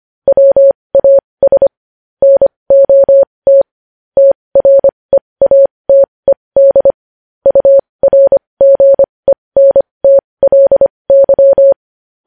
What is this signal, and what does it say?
Morse code, heard as a keyed tone, about 25 wpm, spaced as for 20 wpm, 550 Hz. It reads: WAS NOT TREATED URGENTLY
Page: news-headlines-in-morse-code-at-20-wpm